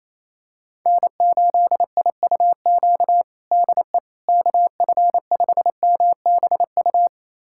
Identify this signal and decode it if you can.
Morse code, heard as a keyed tone, about 28 wpm, 700 Hz. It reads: N8IUQ DE KF5MBU
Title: USA Callsign DE Callsign 28wpm